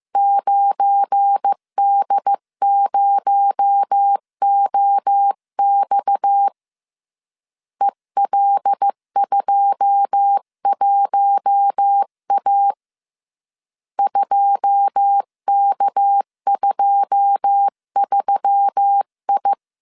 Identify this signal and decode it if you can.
Morse code, heard as a keyed tone, about 15 wpm, 800 Hz. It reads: 9D0OX EL21A 2K23I
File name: CW / DET / 1 9 1500C S B 14wpm 3302